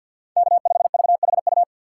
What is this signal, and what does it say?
Morse code, heard as a keyed tone, about 52 wpm, 700 Hz. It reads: K5VHU
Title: USA Call Signs - 2x Character Spacing - 50wpm